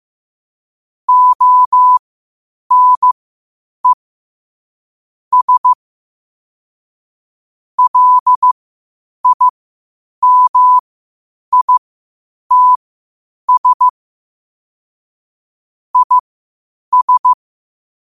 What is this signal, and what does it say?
Morse code, heard as a keyed tone, about 15 wpm, spaced as for 8 wpm, 1000 Hz. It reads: ONE S LIMITS IS
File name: qotdfiles / 8wpm